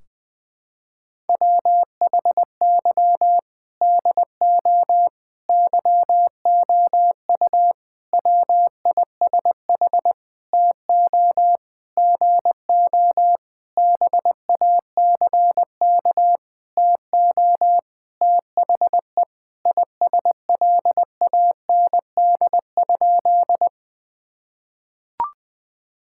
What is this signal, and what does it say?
Morse code, heard as a keyed tone, about 20 wpm, 700 Hz. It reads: WHY DO YOU WISH TO GO BACK TO THE ISLAND?
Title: Sentences from Top 1600 Words 20wpm